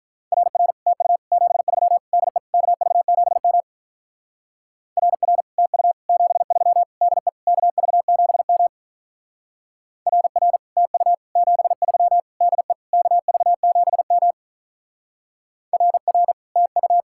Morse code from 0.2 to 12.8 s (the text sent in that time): RR TU 73 DE KV7M RR TU 73 DE KV7M RR TU 73 DE